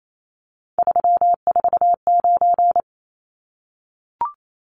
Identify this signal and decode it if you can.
Morse code, heard as a keyed tone, about 28 wpm, 700 Hz. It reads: 349
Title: Numbers - 3-Digit 28wpm